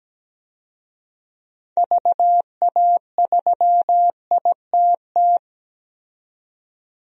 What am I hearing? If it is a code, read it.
Morse code, heard as a keyed tone, about 17 wpm, 700 Hz. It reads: VA3ITT
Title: DXCC-100 Call Signs - 6 Characters - 17wpm